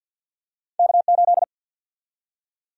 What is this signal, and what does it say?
Morse code, heard as a keyed tone, about 47 wpm, 700 Hz. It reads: K8
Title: USA Call Sign Prefixes - 50wpm